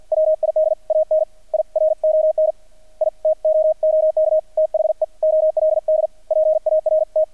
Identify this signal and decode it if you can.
Morse code, heard as a keyed tone, about 50 wpm, 630 Hz, with radio noise. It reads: WID AN I RON I TOOK THE OLD PART
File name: archived / CRSnet